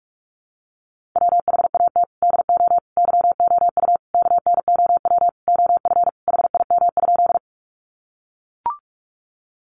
Morse code, heard as a keyed tone, about 44 wpm, 700 Hz. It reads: WHAT DO YOU KNOW OF HIM?
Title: Sentences from Top 100 Words 45wpm